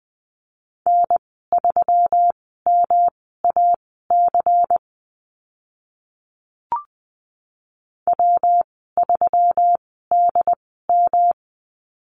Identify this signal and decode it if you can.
Morse code, heard as a keyed tone, about 20 wpm, spaced as for 15 wpm, 700 Hz. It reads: N3MAC W3DM
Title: 161 Lesson 26g - Letter g - Call Signs 20wpm